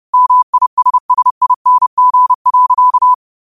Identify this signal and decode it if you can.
Morse code, heard as a keyed tone, about 30 wpm, 1000 Hz. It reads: MISSING.